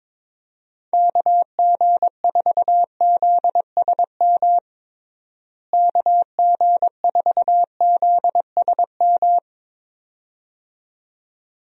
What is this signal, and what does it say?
Morse code, heard as a keyed tone, about 22 wpm, 700 Hz. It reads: KG4ZSM KG4ZSM
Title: USA Call Signs - 6 Characters - Repeated 2x - 22wpm